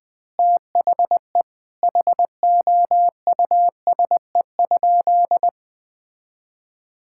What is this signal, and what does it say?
Morse code, heard as a keyed tone, about 20 wpm, 700 Hz. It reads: THE HOUSE?
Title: Sentences from Top 1000 Words - Review Part 5 - 20wpm